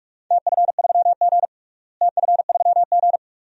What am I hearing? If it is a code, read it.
Morse code, heard as a keyed tone, about 46 wpm, 700 Hz. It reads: TF3G TF3G
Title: DXCC-100 Call Signs - 4 Characters - Repeated 2x - 45wpm